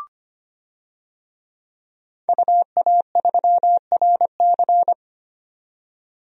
Morse code, heard as a keyed tone, about 25 wpm, 700 Hz. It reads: UA3RC
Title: DXCC-100 Call Signs - 5 Characters - 25wpm